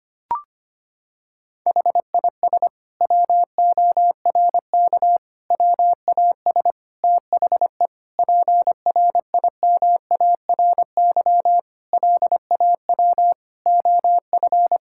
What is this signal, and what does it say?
Morse code, heard as a keyed tone, about 25 wpm, 700 Hz. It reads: HIS WORK WAS THE PRIMARY LAW OF
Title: Sentences from Top 1000 Words - Review Part 5 - 25wpm